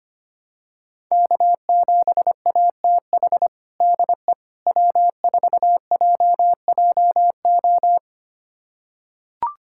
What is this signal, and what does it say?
Morse code, heard as a keyed tone, about 25 wpm, 700 Hz. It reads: K7ATH DE W4JJO
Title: USA Callsign DE Callsign - 5 Characters - 25wpm